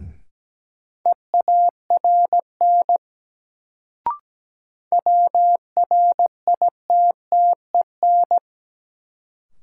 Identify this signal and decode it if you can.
Morse code, heard as a keyed tone, about 17 wpm, 700 Hz. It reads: EARN WRITTEN
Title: Sentences from Top 1500 Words - New Word Drill - 17wpm